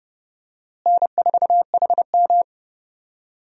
Morse code, heard as a keyed tone, about 30 wpm, 700 Hz. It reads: N4HM
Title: USA Call Signs - 4 Characters - Repeated 3x - 30wpm